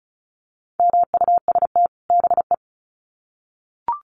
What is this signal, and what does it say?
Morse code, heard as a keyed tone, about 34 wpm, 700 Hz. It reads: MUST BE
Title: Sets of 2 Words 35wpm